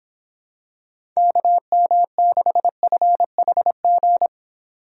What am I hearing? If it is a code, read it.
Morse code, heard as a keyed tone, about 26 wpm, 700 Hz. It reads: KM6FHG